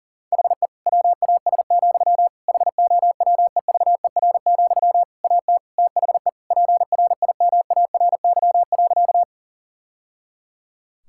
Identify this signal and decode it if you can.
Morse code, heard as a keyed tone, about 40 wpm, 700 Hz. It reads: HE WAS, HOWEVER, AT THE PRIMARY.
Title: Sentences from Top 1000 Words - Review Part 5 - 40wpm